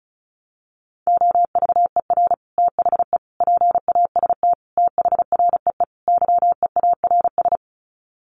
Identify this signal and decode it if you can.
Morse code, heard as a keyed tone, about 34 wpm, 700 Hz. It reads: OVER THE PAST THREE YEARS
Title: Sets of 5 Words Encore2 35wpm